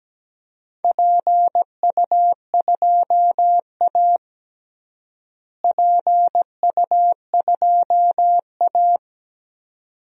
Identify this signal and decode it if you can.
Morse code, heard as a keyed tone, about 17 wpm, 700 Hz. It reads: PU2A PU2A
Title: DXCC-100 Call Signs - 4 Characters - Repeated 2x - 17wpm